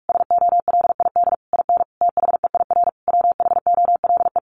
Morse code, heard as a keyed tone, about 45 wpm, 700 Hz. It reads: SOLID IN THEIR WHOLE
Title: Sentences from Top 1500 Words 45wpm